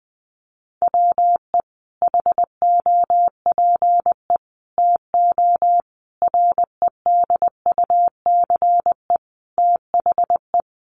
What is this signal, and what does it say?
Morse code, heard as a keyed tone, about 20 wpm, 700 Hz. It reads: WE HOPE TO REDUCE THE
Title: Sentences from Top 1100 Words 20wpm